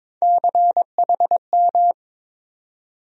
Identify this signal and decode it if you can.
Morse code, heard as a keyed tone, about 21 wpm, 700 Hz. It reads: CHM
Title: USA Call Sign Postfixes - 3 Letters 22wpm